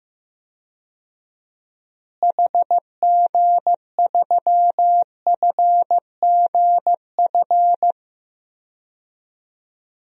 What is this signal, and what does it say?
Morse code, heard as a keyed tone, about 15 wpm, 700 Hz. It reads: HG3FGF
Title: DXCC-100 Call Signs - 6 Characters - Repeated 3x - 15wpm